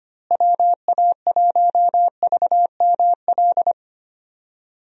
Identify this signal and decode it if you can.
Morse code, heard as a keyed tone, about 25 wpm, 700 Hz. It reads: WA1VML
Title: USA Call Signs - 6 Characters - Repeated 3x - 25wpm